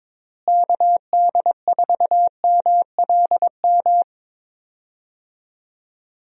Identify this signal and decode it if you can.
Morse code, heard as a keyed tone, about 22 wpm, 700 Hz. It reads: KD4MLM
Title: USA Call Signs - 6 Characters - Repeated 2x - 22wpm